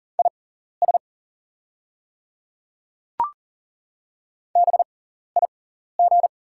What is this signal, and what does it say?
Morse code, heard as a keyed tone, about 40 wpm, spaced as for 14 wpm, 690 Hz. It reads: IS BIG
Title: 6x Character Spacing - Top 200 Words - 40wpm